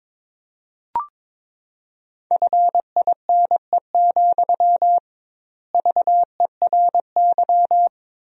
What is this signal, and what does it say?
Morse code, heard as a keyed tone, about 22 wpm, 700 Hz. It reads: FINE, VERY
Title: 2x Word-Spacing - Sentences from Top 500 Words - 22wpm